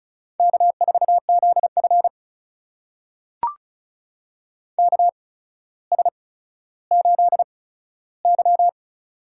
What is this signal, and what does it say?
Morse code, heard as a keyed tone, about 36 wpm, spaced as for 15 wpm, 700 Hz. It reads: K4ZF KS8Y
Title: USA Call Signs - 4 Characters - 8x Character Spacing - 35wpm